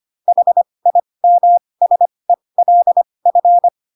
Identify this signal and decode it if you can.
Morse code, heard as a keyed tone, about 25 wpm, spaced as for 20 wpm, 700 Hz. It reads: HIMSELF
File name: Pinocchio / Pinocchio ch2 15 25 ik1hge